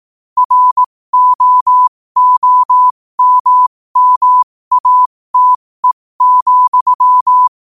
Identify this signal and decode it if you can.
Morse code, heard as a keyed tone, about 18 wpm, spaced as for 15 wpm, 1000 Hz. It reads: ROOMMATE,
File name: qotdfiles / 15wpm